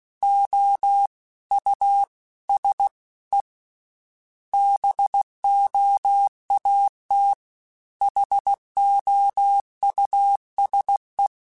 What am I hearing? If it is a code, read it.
Morse code, heard as a keyed tone, about 16 wpm, 800 Hz. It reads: O U S E BOAT HOUSE
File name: General / General-mp3s / Houses @15